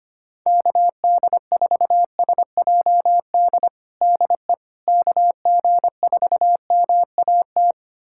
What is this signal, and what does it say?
Morse code, heard as a keyed tone, about 25 wpm, 700 Hz. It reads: KD4SJD DE KG4MAT
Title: USA Callsign DE Callsign 25wpm